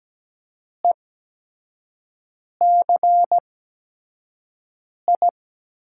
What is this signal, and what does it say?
Morse code, heard as a keyed tone, about 17 wpm, spaced as for 5 wpm, 700 Hz. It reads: ECI
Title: USA Call Signs - 8x Character Spacing - 17wpm